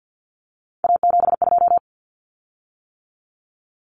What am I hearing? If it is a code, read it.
Morse code, heard as a keyed tone, about 50 wpm, 700 Hz. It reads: A72
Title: Common Cut Numbers - 3-Digit 50wpm